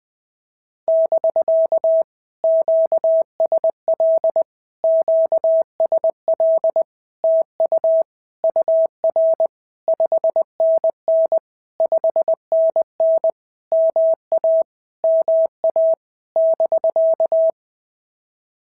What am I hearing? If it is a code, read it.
Morse code, heard as a keyed tone, about 20 wpm, 650 Hz. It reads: <BK> QSL QSL TU UR 5NN 5NN MA MA <BK>
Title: POTA QSO Practice - 20wpm